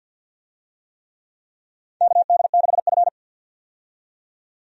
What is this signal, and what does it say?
Morse code, heard as a keyed tone, about 52 wpm, 700 Hz. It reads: KD6F
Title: USA Call Signs - 4 Characters - 50wpm